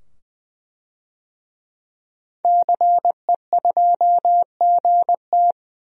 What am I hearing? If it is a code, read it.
Morse code, heard as a keyed tone, about 20 wpm, 700 Hz. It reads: CE2GT
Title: DXCC-100 Call Signs - 5 Characters - 20wpm